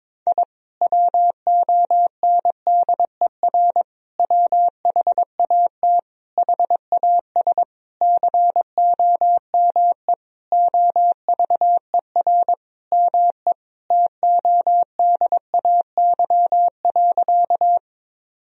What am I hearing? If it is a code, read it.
Morse code, heard as a keyed tone, about 22 wpm, 700 Hz. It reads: I WONDER WHAT HAS COME OVER ME TODAY.
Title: Sentences from Top 1000 Words - Review Part 4 - 22wpm